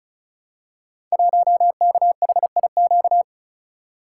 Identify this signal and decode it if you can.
Morse code, heard as a keyed tone, about 36 wpm, 700 Hz. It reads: 1KHIQ
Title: Code Groups - 5 Letter:Numbers - 35wpm